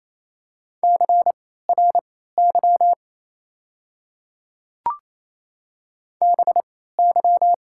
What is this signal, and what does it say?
Morse code, heard as a keyed tone, about 28 wpm, spaced as for 16 wpm, 700 Hz. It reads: CRY BY
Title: 3x Character Spacing - Top 500 Words - 28wpm